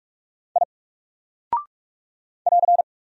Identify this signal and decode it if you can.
Morse code, heard as a keyed tone, about 46 wpm, 700 Hz. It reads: I <AR>
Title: Single Character (Letter:Number:Punctuation:Prosign) Alt 45wpm